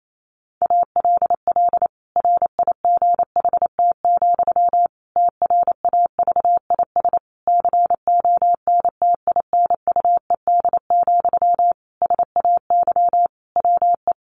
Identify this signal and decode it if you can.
Morse code, heard as a keyed tone, about 28 wpm, 700 Hz. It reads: ALL RIGHT, TRAVIS CONTINUED, SAY WE
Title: A Sound Of Thunder 28wpm